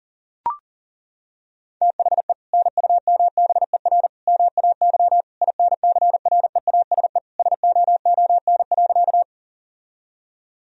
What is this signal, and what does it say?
Morse code, heard as a keyed tone, about 40 wpm, 700 Hz. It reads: THE NUMBER MAY INCREASE SOON.
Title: Sentences from Top 1000 Words - Review Part 4 - 40wpm